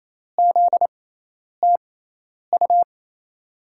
Morse code, heard as a keyed tone, about 28 wpm, spaced as for 10 wpm, 700 Hz. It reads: ZTU
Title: USA Call Signs - 6x Character Spacing - 28wpm